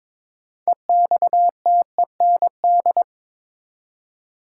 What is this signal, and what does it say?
Morse code, heard as a keyed tone, about 22 wpm, 700 Hz. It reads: EXTEND
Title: Six Letter Words - 22wpm